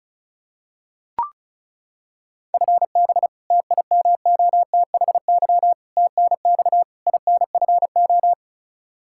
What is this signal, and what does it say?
Morse code, heard as a keyed tone, about 36 wpm, 700 Hz. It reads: FB TIMOTHY TNX INFO
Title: K1USN SST QSO Practice 35wpm